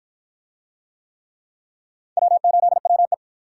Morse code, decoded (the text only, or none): W8WE